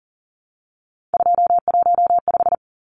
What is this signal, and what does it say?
Morse code, heard as a keyed tone, about 37 wpm, 700 Hz. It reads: 215
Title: Numbers - 3-Digit 40wpm